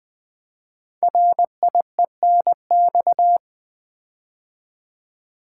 Morse code, heard as a keyed tone, about 20 wpm, 700 Hz. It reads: RIENX
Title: Code Groups - 5 Letters - 20wpm